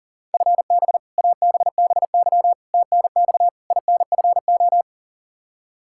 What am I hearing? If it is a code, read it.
Morse code, heard as a keyed tone, about 40 wpm, 700 Hz. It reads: FB ABBY TNX INFO
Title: Practice QSO - Exchange 3 - More Info - 40wpm